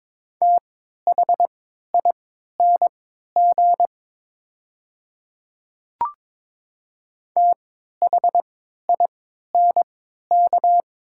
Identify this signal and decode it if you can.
Morse code, heard as a keyed tone, about 22 wpm, spaced as for 12 wpm, 700 Hz. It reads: THING THINK
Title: 3x Character Spacing - Top 500 Words - 22wpm